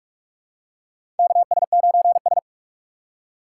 Morse code, heard as a keyed tone, about 48 wpm, 700 Hz. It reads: KS9S
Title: USA Call Signs - 4 Characters - 2x Character Spacing - 45wpm